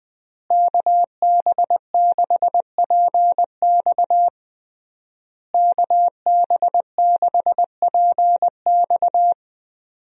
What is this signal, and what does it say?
Morse code, heard as a keyed tone, about 21 wpm, 700 Hz. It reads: KB6PX KB6PX
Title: USA Call Signs - 5 Characters - Repeated 3x - 20wpm